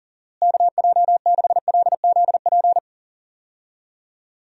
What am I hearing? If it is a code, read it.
Morse code, heard as a keyed tone, about 40 wpm, 700 Hz. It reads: KJ6LZP
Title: USA Call Signs - 6 Characters - Repeated 2x - 40wpm